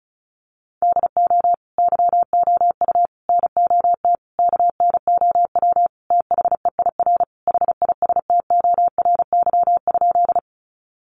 Morse code, heard as a keyed tone, about 34 wpm, 700 Hz. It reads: DO YOU NOT KNOW THEIR HISTORY?
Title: Sentences from Top 600 Words 35wpm